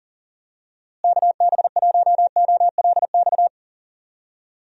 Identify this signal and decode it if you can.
Morse code, heard as a keyed tone, about 40 wpm, 700 Hz. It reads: KB1OLX
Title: USA Call Signs - 6 Characters - Repeated 2x - 40wpm